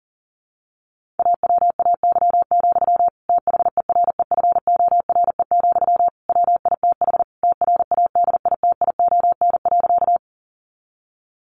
Morse code, heard as a keyed tone, about 40 wpm, 700 Hz. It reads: AWAY, THEREFORE, WITH TRADITION.
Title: Sentences from Top 1500 Words 40wpm